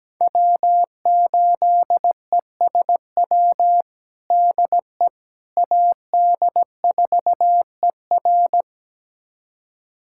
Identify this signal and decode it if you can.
Morse code, heard as a keyed tone, about 17 wpm, 700 Hz. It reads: W8ESW DE AD4ER
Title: USA Callsign DE Callsign - 5 Characters - 17wpm